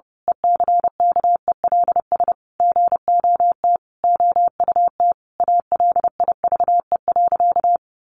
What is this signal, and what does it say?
Morse code, heard as a keyed tone, about 30 wpm, 700 Hz. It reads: ECKELS GOT OUT ALIVE.
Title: A Sound Of Thunder 30wpm